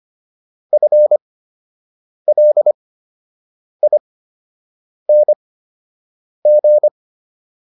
Morse code, heard as a keyed tone, about 25 wpm, spaced as for 7 wpm, 600 Hz. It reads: FLING